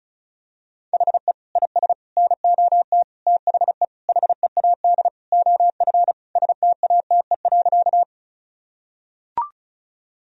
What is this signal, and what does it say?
Morse code, heard as a keyed tone, about 36 wpm, 700 Hz. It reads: HE IS NOT THE HEAD OF STATE.
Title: Sentences from Top 1000 Words - Review Part 2 - 35wpm